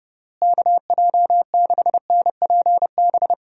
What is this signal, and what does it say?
Morse code, heard as a keyed tone, about 32 wpm, 700 Hz. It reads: KJ6NPB
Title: USA Call Signs - 6 Characters - Repeated 3x - 30wpm